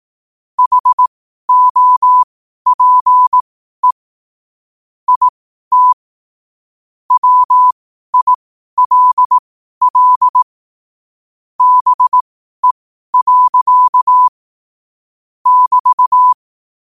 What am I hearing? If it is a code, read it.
Morse code, heard as a keyed tone, about 18 wpm, spaced as for 12 wpm, 1000 Hz. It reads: HOPE IT WILL BE. <BT>
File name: qotdfiles / 12wpm